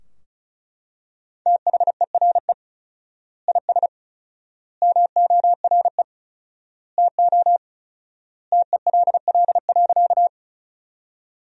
Morse code, heard as a keyed tone, about 36 wpm, 700 Hz. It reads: THERE IS MORE TO TELL.
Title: 4x Word-Spacing - Sentences from Top 200 Words - 35wpm